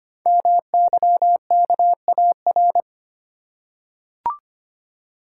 Morse code, heard as a keyed tone, about 25 wpm, 700 Hz. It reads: MYKAR
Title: Code Groups - 5 Letters - 25wpm